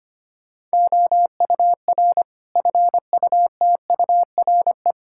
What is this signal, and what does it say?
Morse code, heard as a keyed tone, about 25 wpm, 700 Hz. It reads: OUR FUTURE
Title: Sentences from Top 1000 Words - Review Part 5 - 25wpm